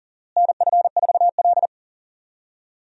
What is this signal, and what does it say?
Morse code, heard as a keyed tone, about 40 wpm, 700 Hz. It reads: NF4L
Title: USA Call Signs - 4 Characters - Repeated 3x - 40wpm